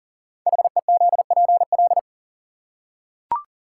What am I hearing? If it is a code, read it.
Morse code, heard as a keyed tone, about 42 wpm, 690 Hz. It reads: HEZPL
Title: Code Groups - 5 Letters - 40wpm